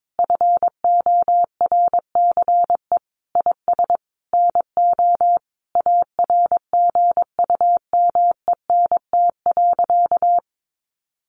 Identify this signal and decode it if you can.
Morse code, heard as a keyed tone, about 22 wpm, 700 Hz. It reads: FORCE IS NO ARGUMENT.
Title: Sentences from Top 1400 Words 22wpm